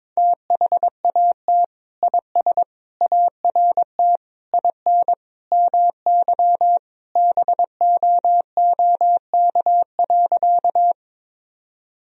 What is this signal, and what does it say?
Morse code, heard as a keyed tone, about 22 wpm, 700 Hz. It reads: THAT IS ART IN MY BOOK.
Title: Sentences from Top 1000 Words - Review Part 3 - 22wpm